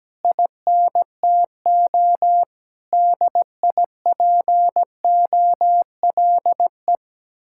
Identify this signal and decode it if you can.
Morse code, heard as a keyed tone, about 17 wpm, 700 Hz. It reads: INTO DIPOLE
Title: Speed-Racing Common QSO Phrases - 17wpm